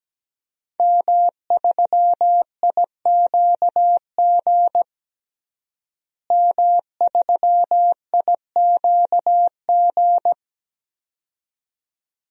Morse code, heard as a keyed tone, about 17 wpm, 700 Hz. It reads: M3IQG M3IQG
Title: DXCC-100 Call Signs - 5 Characters - Repeated 2x - 17wpm